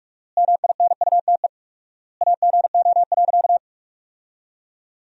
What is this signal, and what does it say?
Morse code, heard as a keyed tone, about 47 wpm, 700 Hz. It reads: MINUTE AGO.
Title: 4x Word-Spacing - Sentences from Top 500 Words - 45wpm